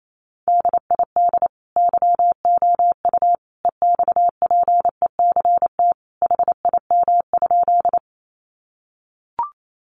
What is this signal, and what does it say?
Morse code, heard as a keyed tone, about 28 wpm, 700 Hz. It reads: DID YOU EXPECT HIM?
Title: Sentences from Top 1100 Words 28wpm